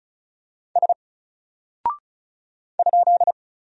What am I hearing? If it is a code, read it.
Morse code, heard as a keyed tone, about 36 wpm, 700 Hz. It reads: S ?
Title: Single Character (Letter:Number:Punctuation:Prosign) Alt 35wpm